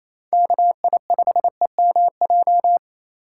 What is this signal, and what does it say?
Morse code, heard as a keyed tone, about 27 wpm, 700 Hz. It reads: KI5EMJ
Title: USA Call Signs - 6 Characters - 3x Character Spacing - 28wpm